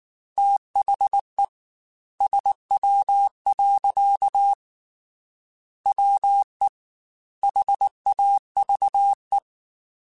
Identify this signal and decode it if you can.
Morse code, heard as a keyed tone, about 19 wpm, 800 Hz. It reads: THE SW. WE HAVE